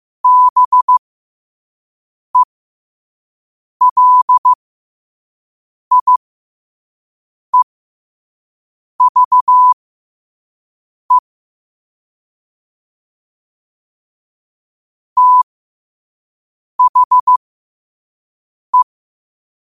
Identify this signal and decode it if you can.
Morse code, heard as a keyed tone, about 15 wpm, spaced as for 5 wpm, 1000 Hz. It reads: BELIEVE THE
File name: qotdfiles / 5wpm